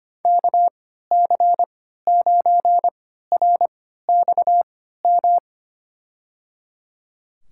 Morse code, heard as a keyed tone, about 25 wpm, spaced as for 14 wpm, 700 Hz. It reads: KC9RXM
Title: USA Call Signs - 6 Characters - 3x Character Spacing - 25wpm